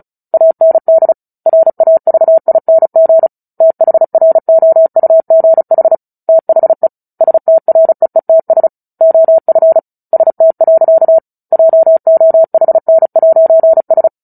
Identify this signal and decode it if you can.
Morse code, heard as a keyed tone, about 34 wpm, 650 Hz. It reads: AND RAVING THROUGH THE STREETS OF ST. JOHN'S
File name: Book Two / 09-Chapter 9 - 35 WPM